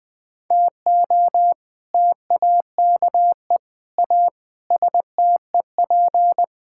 Morse code, heard as a keyed tone, about 21 wpm, 700 Hz. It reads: TO TAKE A STEP
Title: Sets of 6 Words - 20wpm